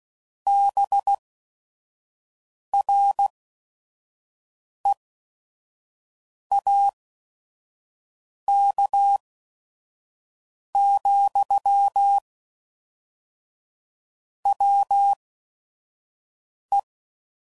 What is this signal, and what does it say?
Morse code, heard as a keyed tone, about 16 wpm, spaced as for 5 wpm, 800 Hz. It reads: BREAK, WE
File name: CW Code Groups @5WPM / QSOs / QSO WA7QOX 1